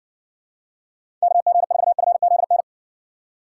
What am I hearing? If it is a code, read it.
Morse code, heard as a keyed tone, about 60 wpm, 700 Hz. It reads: KG4FZN